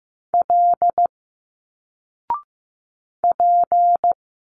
Morse code, heard as a keyed tone, about 15 wpm, 700 Hz. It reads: L P
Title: Single Letters 15wpm